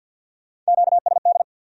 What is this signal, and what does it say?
Morse code, heard as a keyed tone, about 66 wpm, spaced as for 58 wpm, 700 Hz. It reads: QSD
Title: Speed-Racing Q-Codes - 50wpm